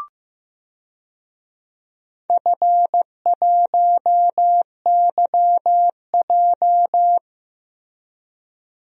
Morse code, heard as a keyed tone, about 15 wpm, 700 Hz. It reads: F1YJ
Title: DXCC-100 Call Signs - 4 Characters - Repeated 2x - 15wpm